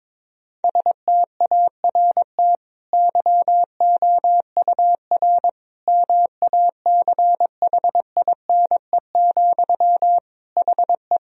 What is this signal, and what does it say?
Morse code, heard as a keyed tone, about 22 wpm, 700 Hz. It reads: START YOUR MACHINE, HE
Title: Sentences from Top 1000 Words - Review Part 2 - 22wpm